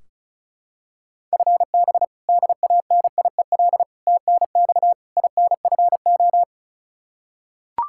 Morse code, heard as a keyed tone, about 36 wpm, 700 Hz. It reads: FB DANIEL TNX INFO
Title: Practice QSO - Exchange 3 - More Info - 35wpm